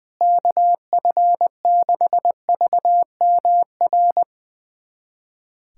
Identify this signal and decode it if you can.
Morse code, heard as a keyed tone, about 20 wpm, 700 Hz. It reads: KF6VMR
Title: USA Callsign DE Callsign - 6 Characters - 20wpm